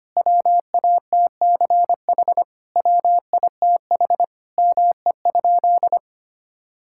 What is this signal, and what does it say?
Morse code, heard as a keyed tone, about 25 wpm, 700 Hz. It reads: WATCH WITH ME?
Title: Sentences from Top 1000 Words - Review Part 5 - 25wpm